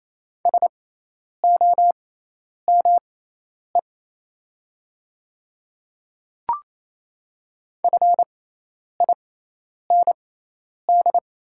Morse code, heard as a keyed tone, about 27 wpm, spaced as for 10 wpm, 700 Hz. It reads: SOME FIND
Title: 6x Character Spacing - Top 100 Words - 28wpm